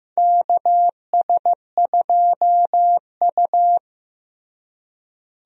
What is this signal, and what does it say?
Morse code, heard as a keyed tone, about 15 wpm, 700 Hz. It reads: KS2U
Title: USA Call Signs - 4 Characters - Repeated 2x - 15wpm